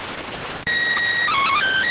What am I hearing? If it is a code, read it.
SSTV (slow-scan television), PD 290.